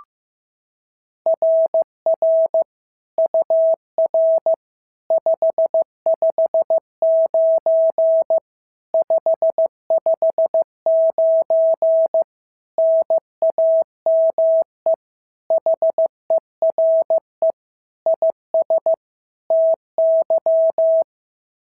Morse code, heard as a keyed tone, about 14 wpm, 650 Hz. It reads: RR UR 559 559 NAME HERE IS TY
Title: SOTA QSO Practice 15wpm